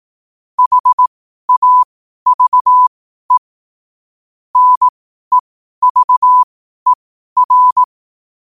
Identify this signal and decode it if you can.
Morse code, heard as a keyed tone, about 18 wpm, spaced as for 12 wpm, 1000 Hz. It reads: HAVE NEVER
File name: qotdfiles / 12wpm